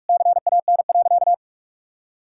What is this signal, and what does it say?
Morse code, heard as a keyed tone, about 48 wpm, 700 Hz. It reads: CAN.
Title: Sentences from Top 1600 Words 45wpm